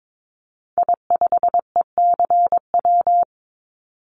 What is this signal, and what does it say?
Morse code, heard as a keyed tone, about 22 wpm, 700 Hz. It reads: I5ECW
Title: Call Signs - 22wpm